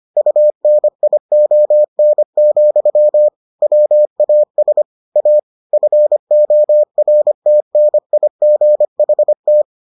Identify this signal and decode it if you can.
Morse code, heard as a keyed tone, about 25 wpm, 600 Hz. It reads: UNION, WAS A FORTNIGHT